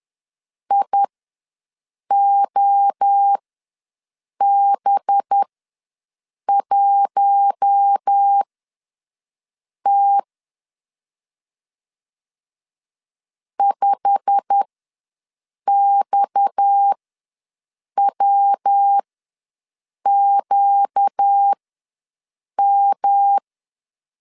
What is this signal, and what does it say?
Morse code, heard as a keyed tone, about 10 wpm, spaced as for 6 wpm, 800 Hz. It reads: IOB1T 5XWQM
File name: LEZIONI / 8 YJ45 / 8 YJ45 ws 10wpm 3843